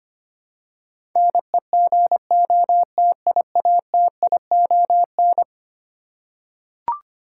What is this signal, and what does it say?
Morse code, heard as a keyed tone, about 25 wpm, 700 Hz. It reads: NEGOTIATION
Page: morse-code-ninja-practice-25wpm